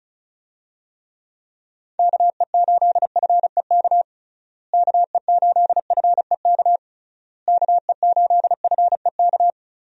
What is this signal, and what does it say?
Morse code, heard as a keyed tone, about 34 wpm, 700 Hz. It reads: KE8FEK KE8FEK KE8FEK
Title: USA Call Signs - 6 Characters - Repeated 3x - 35wpm